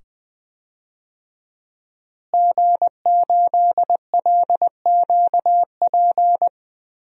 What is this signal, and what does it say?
Morse code, heard as a keyed tone, about 20 wpm, 700 Hz. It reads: G8LQP
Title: DXCC-100 Call Signs - 5 Characters - Repeated 2x - 20wpm